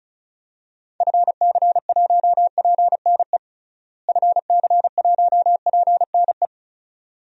Morse code, heard as a keyed tone, about 36 wpm, 700 Hz. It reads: FC1PNE FC1PNE
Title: DXCC-100 Call Signs - 6 Characters - Repeated 2x - 35wpm